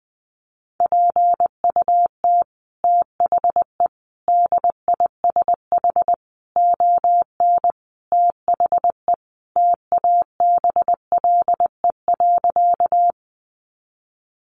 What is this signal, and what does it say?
Morse code, heard as a keyed tone, about 20 wpm, 700 Hz. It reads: PUT THE DISH ON THE TABLE.
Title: Sentences from Top 1300 Words 20wpm